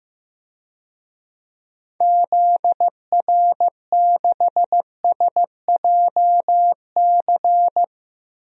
Morse code, heard as a keyed tone, about 15 wpm, 700 Hz. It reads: ZR6SJC
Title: DXCC-100 Call Signs - 6 Characters - Repeated 3x - 15wpm